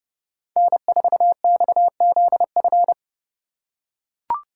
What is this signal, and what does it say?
Morse code, heard as a keyed tone, about 28 wpm, 700 Hz. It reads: N4XZF